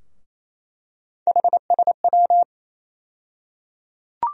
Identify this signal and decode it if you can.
Morse code, heard as a keyed tone, about 28 wpm, 700 Hz. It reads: HSW
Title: USA Call Sign Postfixes - 3 Letters 28wpm